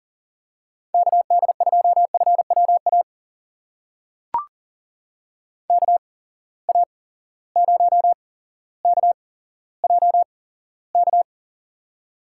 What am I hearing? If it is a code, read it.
Morse code, heard as a keyed tone, about 40 wpm, spaced as for 17 wpm, 700 Hz. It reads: KD2FWA KA0KJK
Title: USA Call Signs - 6 Characters - 8x Character Spacing - 40wpm